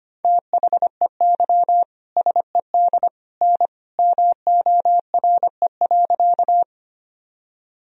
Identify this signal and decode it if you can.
Morse code, heard as a keyed tone, about 25 wpm, 700 Hz. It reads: THEY SED N MORE.
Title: Sentences from Top 100 Words and Abbreviations 25wpm